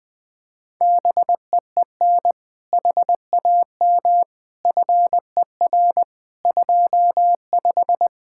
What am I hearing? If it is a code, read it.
Morse code, heard as a keyed tone, about 21 wpm, 700 Hz. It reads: BEEN HAM FER 25
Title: K1USN SST QSO Practice 20wpm